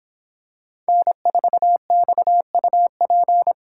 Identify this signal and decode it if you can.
Morse code, heard as a keyed tone, about 26 wpm, 700 Hz. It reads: N4XUP